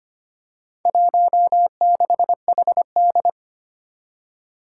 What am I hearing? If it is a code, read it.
Morse code, heard as a keyed tone, about 25 wpm, 700 Hz. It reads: 16HD
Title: Four Challenging Letter-Number Combinations 25wpm